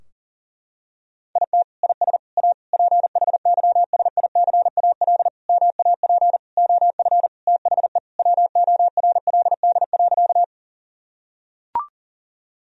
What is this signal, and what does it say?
Morse code, heard as a keyed tone, about 40 wpm, 700 Hz. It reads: IT IS A PHYSICAL MAP OF THE WORLD.
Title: Sentences from Top 1000 Words - Review Part 4 - 40wpm